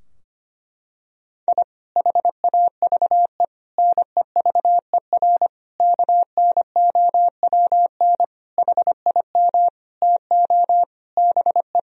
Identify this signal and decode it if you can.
Morse code, heard as a keyed tone, about 25 wpm, 700 Hz. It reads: I HAVE NEVER KNOWN HIM TO BE
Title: Sentences from Top 1000 Words - Review Part 3 - 25wpm